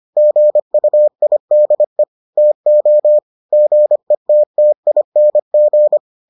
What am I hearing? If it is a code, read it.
Morse code, heard as a keyed tone, about 25 wpm, 600 Hz. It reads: GUIDE TO GETTING